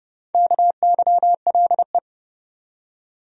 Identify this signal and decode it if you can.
Morse code, heard as a keyed tone, about 31 wpm, 700 Hz. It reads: KYLE
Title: USA Common Names - Max 5 Letters - 30wpm